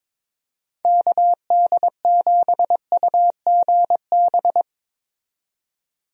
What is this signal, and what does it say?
Morse code, heard as a keyed tone, about 22 wpm, 700 Hz. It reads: KD7UGB
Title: USA Call Signs - 6 Characters - 22wpm